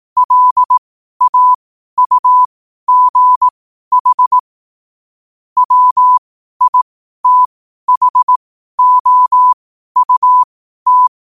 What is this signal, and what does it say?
Morse code, heard as a keyed tone, about 18 wpm, spaced as for 12 wpm, 1000 Hz. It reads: LAUGH WITHOUT
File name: qotdfiles / 12wpm